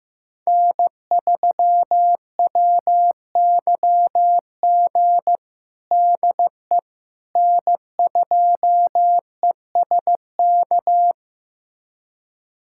Morse code, heard as a keyed tone, about 15 wpm, 700 Hz. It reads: N3WYG DE N2ESK
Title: USA Callsign DE Callsign - 5 Characters - 15wpm